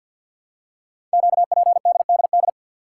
Morse code, heard as a keyed tone, about 48 wpm, 690 Hz. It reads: QPDDD
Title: Code Groups - 5 Letters - 50wpm